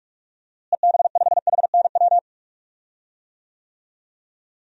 Morse code, heard as a keyed tone, about 41 wpm, 700 Hz. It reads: EB5HNW
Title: DXCC-100 Call Signs - 6 Characters - Repeated 2x - 45wpm